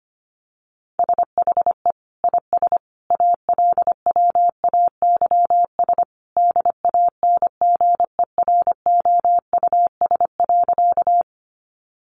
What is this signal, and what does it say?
Morse code, heard as a keyed tone, about 25 wpm, 700 Hz. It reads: SHE IS ALWAYS DANGEROUS.
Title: Sentences from Top 1200 Words 25wpm